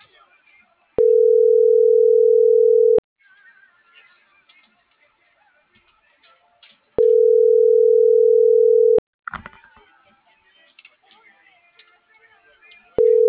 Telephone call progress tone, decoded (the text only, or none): ringback tone